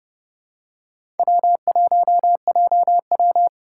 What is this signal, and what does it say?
Morse code, heard as a keyed tone, about 31 wpm, 700 Hz. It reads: W1JW
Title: USA Call Signs - 4 Characters - 4x Character Spacing - 30wpm